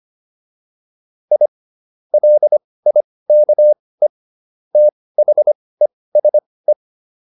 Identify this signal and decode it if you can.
Morse code, heard as a keyed tone, about 25 wpm, spaced as for 18 wpm, 600 Hz. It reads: I LIKE THESE